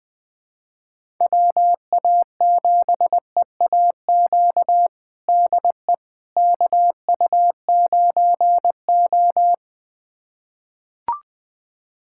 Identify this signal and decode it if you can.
Morse code, heard as a keyed tone, about 20 wpm, 700 Hz. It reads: WA7EAQ DE KU9O
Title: USA Callsign DE Callsign 20wpm